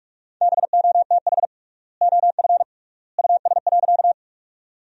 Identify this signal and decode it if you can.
Morse code, heard as a keyed tone, about 46 wpm, 700 Hz. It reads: BOTH OF US.
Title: 3x Word-Spacing - Sentences from Top 300 Words - 45wpm